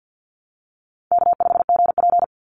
Morse code, heard as a keyed tone, about 50 wpm, 700 Hz. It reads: K5GP
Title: Easy Call Signs - 50wpm